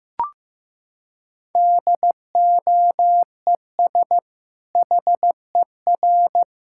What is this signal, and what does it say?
Morse code, heard as a keyed tone, about 15 wpm, 700 Hz. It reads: DOES HER
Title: Sentences from Top 1000 Words - Review Part 6 - 15wpm